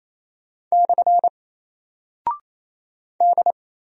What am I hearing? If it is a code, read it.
Morse code, heard as a keyed tone, about 27 wpm, 700 Hz. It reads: / D
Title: Single Character (Letter:Number:Punctuation:Prosign) 28wpm